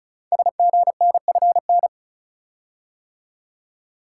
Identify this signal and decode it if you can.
Morse code, heard as a keyed tone, about 36 wpm, 700 Hz. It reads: SGNFN
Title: Code Groups - 5 Letters - 35wpm